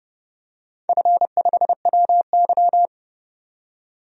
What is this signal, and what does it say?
Morse code, heard as a keyed tone, about 31 wpm, 700 Hz. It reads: F5WY
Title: DXCC-100 Call Signs - 4 Characters - 30wpm